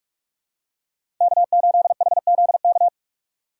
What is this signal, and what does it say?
Morse code, heard as a keyed tone, about 42 wpm, 700 Hz. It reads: K8HZK
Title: USA Call Signs - 5 Characters - 2x Character Spacing - 45wpm